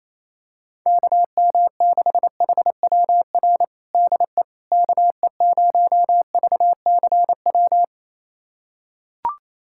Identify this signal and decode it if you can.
Morse code, heard as a keyed tone, about 28 wpm, 700 Hz. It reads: KM6HWR DE KE0VCW
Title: USA Callsign DE Callsign - 6 Characters - 28wpm